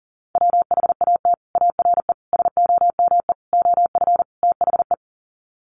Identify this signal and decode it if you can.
Morse code, heard as a keyed tone, about 37 wpm, 700 Hz. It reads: WHAT ARE SOME OF THE
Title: Sets of 5 Words Encore 40wpm